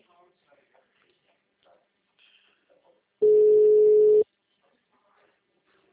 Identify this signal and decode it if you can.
Ringback tone (Europe, 425 Hz), with speech.
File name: out- -2007- - - .3